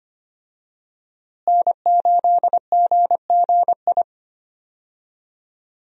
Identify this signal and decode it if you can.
Morse code, heard as a keyed tone, about 25 wpm, 700 Hz. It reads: N8GGI
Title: USA Call Signs - 5 Characters - Repeated 2x - 25wpm